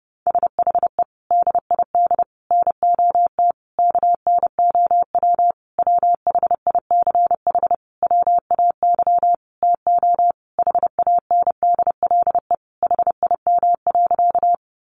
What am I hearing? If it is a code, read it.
Morse code, heard as a keyed tone, about 30 wpm, 700 Hz. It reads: SHE DID NOT KNOW WHICH WAY TO HANDLE HIM.
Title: Sentences from Top 1100 Words 30wpm